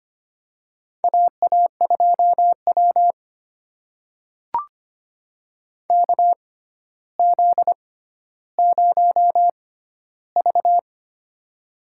Morse code, heard as a keyed tone, about 25 wpm, spaced as for 13 wpm, 700 Hz. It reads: AA2W KZ0V
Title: USA Call Signs - 4 Characters - 6x Character Spacing - 25wpm